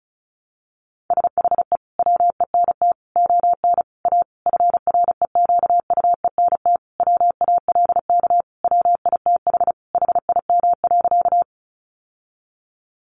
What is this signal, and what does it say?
Morse code, heard as a keyed tone, about 34 wpm, 700 Hz. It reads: SHE WENT ON A FREQUENT WALK WITH HIM.
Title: Sentences from Top 1400 Words 35wpm